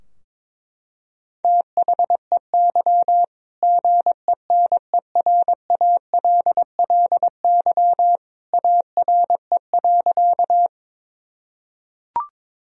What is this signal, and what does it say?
Morse code, heard as a keyed tone, about 22 wpm, 700 Hz. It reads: THEY GENERALLY ARE.
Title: Sentences from Top 1000 Words - Review Part 4 - 22wpm